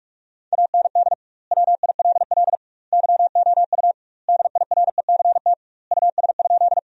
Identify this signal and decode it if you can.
Morse code, heard as a keyed tone, about 46 wpm, 700 Hz. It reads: AND WILL YOU DIRECT US?
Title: 2x Word-Spacing - Sentences from Top 400 Words - 45wpm